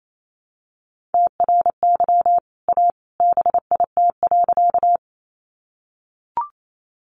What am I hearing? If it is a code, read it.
Morse code, heard as a keyed tone, about 28 wpm, 690 Hz. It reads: TRY A BIT.
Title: Sentences from Top 900 Words 28wpm